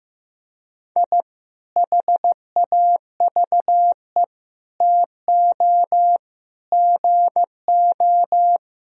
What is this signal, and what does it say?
Morse code, heard as a keyed tone, about 16 wpm, 700 Hz. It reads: I HAVE TO GO
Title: Sentences from Top 1000 Words - Review Part 4 - 15wpm